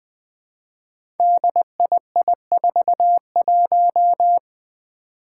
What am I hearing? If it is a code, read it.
Morse code, heard as a keyed tone, about 20 wpm, 700 Hz. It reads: DII41
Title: Five Challenging Letter-Number Combinations 20wpm